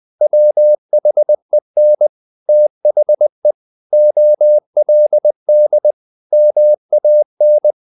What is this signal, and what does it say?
Morse code, heard as a keyed tone, about 20 wpm, 600 Hz. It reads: WHEN THE OLD MAN
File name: the old man and the sea / the old man and the sea part 1 20wpm